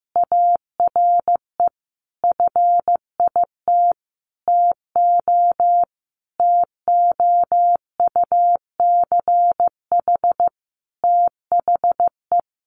Morse code, heard as a keyed tone, about 15 wpm, 700 Hz. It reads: ARE FIT TO TOUCH THE
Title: Sentences from Top 1200 Words 15wpm